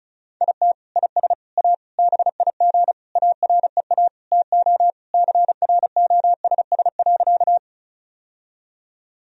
Morse code, heard as a keyed tone, about 36 wpm, 700 Hz. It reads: IT IS A BIG AREA TO CROSS.
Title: Sentences from Top 1000 Words - Review Part 2 - 35wpm